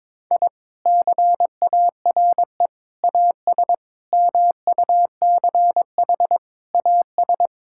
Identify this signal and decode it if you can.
Morse code, heard as a keyed tone, about 21 wpm, 700 Hz. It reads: I CARE AS MUCH AS
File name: practice / 22wpm / Sentences with 3-Word Phrases - 22wpm